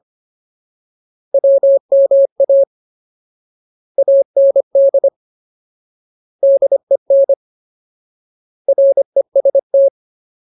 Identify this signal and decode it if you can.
Morse code, heard as a keyed tone, about 25 wpm, 550 Hz. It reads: WMA AND DEN REST
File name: Ultimate Word List / 25WPM Ultimate